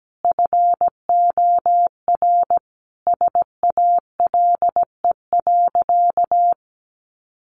Morse code, heard as a keyed tone, about 17 wpm, 700 Hz. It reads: FOR SALE.
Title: Sentences from Top 1400 Words 17wpm